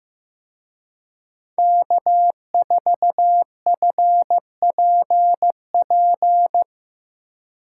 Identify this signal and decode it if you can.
Morse code, heard as a keyed tone, about 17 wpm, 700 Hz. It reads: K4FPP
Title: Speed Racing - USA Call Signs 15wpm